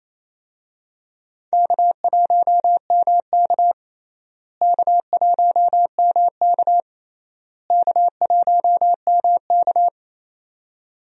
Morse code, heard as a keyed tone, about 28 wpm, 700 Hz. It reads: K1MK K1MK K1MK
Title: USA Call Signs - 4 Characters - Repeated 3x - 28wpm